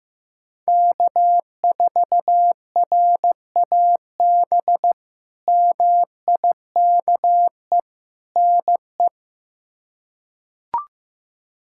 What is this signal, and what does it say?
Morse code, heard as a keyed tone, about 16 wpm, 700 Hz. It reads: K4RAB MIKE NE
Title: K1USN SST QSO Practice - 15wpm